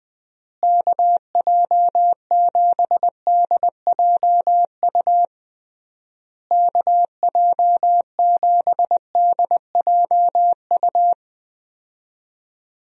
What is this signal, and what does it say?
Morse code, heard as a keyed tone, about 20 wpm, 700 Hz. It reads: KJ7DJU KJ7DJU
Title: USA Call Signs - 6 Characters - Repeated 3x - 20wpm